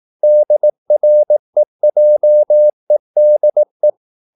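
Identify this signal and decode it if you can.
Morse code, heard as a keyed tone, about 18 wpm, 600 Hz. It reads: DREJEDE
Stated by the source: Slow (QRS) Morse practice at 18 wpm, CW med Gnister.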